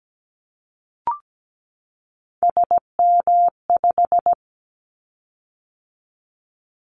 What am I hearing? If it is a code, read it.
Morse code, heard as a keyed tone, about 17 wpm, 700 Hz. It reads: SM5
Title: Callsign Prefixes for Top 25 DXCC - 17wpm